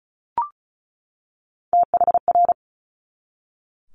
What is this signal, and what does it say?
Morse code, heard as a keyed tone, about 36 wpm, 700 Hz. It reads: THR
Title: Multi-letter Phonemes 35wpm